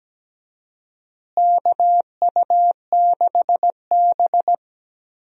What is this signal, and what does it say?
Morse code, heard as a keyed tone, about 17 wpm, 700 Hz. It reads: KU6B
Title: USA Call Signs - 4 Characters - Repeated 3x - 17wpm